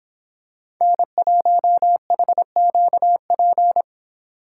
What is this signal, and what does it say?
Morse code, heard as a keyed tone, about 27 wpm, 700 Hz. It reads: N1HQP
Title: Speed Racing - USA 1x3 Call Signs 22wpm